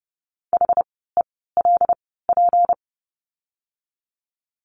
Morse code, heard as a keyed tone, about 30 wpm, spaced as for 17 wpm, 700 Hz. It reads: HELP
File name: course / 30:10wpm / 230 Lesson 37f - Letter x - Words 30wpm